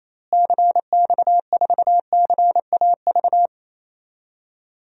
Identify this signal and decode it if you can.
Morse code, heard as a keyed tone, about 28 wpm, 700 Hz. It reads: CX4CAV
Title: DXCC-100 Call Signs - 6 Characters - Repeated 3x - 28wpm